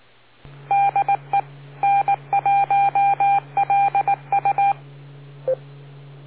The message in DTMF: none